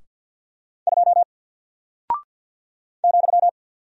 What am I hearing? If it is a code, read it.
Morse code, heard as a keyed tone, about 48 wpm, 700 Hz. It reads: 2 ,
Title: Single Character (Letter:Number:Punctuation:Prosign) Alt 50wpm